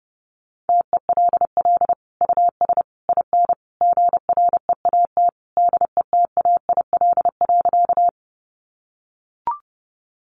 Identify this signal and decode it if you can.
Morse code, heard as a keyed tone, about 30 wpm, 700 Hz. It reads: TELL US IN GREAT DETAIL.
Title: Sentences from Top 1500 Words 30wpm